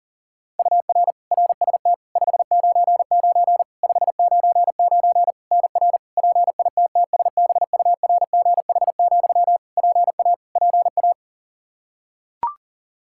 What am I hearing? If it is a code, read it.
Morse code, heard as a keyed tone, about 40 wpm, 700 Hz. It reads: UR RST 599 599 NR PITTSBURGH, PA PA
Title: Practice QSO - Exchange 1 - Beginning - 40wpm